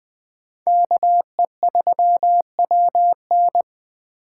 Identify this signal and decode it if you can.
Morse code, heard as a keyed tone, about 21 wpm, 700 Hz. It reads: KE3WN